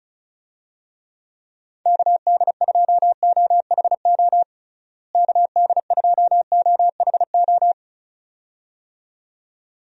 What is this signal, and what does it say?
Morse code, heard as a keyed tone, about 36 wpm, 700 Hz. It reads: KD2OHO KD2OHO
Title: USA Call Signs - 6 Characters - Repeated 2x - 35wpm